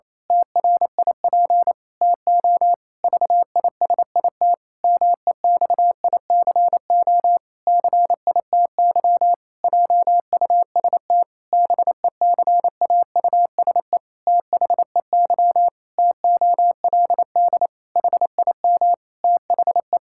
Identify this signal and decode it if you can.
Morse code, heard as a keyed tone, about 28 wpm, 700 Hz. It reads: TRIP TO VISIT MEXICO CITY JUST BECAUSE THEY TOLD HIM THE
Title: Pangram Challenge v2 - 28wpm